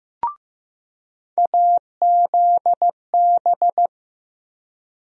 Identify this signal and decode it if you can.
Morse code, heard as a keyed tone, about 15 wpm, 700 Hz. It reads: AZB
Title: USA Call Sign Postfixes - 3 Letters 15wpm